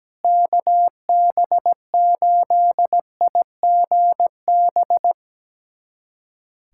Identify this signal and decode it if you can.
Morse code, heard as a keyed tone, about 17 wpm, 700 Hz. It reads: KB8IGB
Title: USA Callsign DE Callsign - 6 Characters - 17wpm